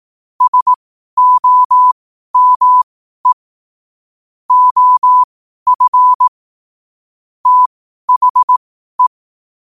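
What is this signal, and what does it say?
Morse code, heard as a keyed tone, about 18 wpm, spaced as for 12 wpm, 1000 Hz. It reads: SOME OF THE
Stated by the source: Morse code sent at 12 words per minute.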